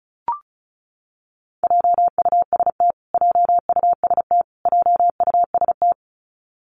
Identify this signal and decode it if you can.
Morse code, heard as a keyed tone, about 34 wpm, 700 Hz. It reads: JUST JUST JUST
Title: Top 300 Words Repeated 35wpm